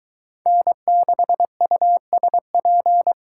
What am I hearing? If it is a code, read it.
Morse code, heard as a keyed tone, about 24 wpm, 700 Hz. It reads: N6USP